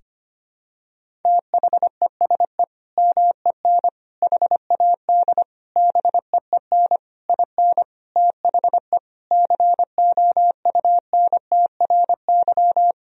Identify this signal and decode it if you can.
Morse code, heard as a keyed tone, about 25 wpm, 700 Hz. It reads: THESE MEN HAD BEEN IN THE COUNTRY
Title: Sentences from Top 1000 Words - Review Part 1 - 25wpm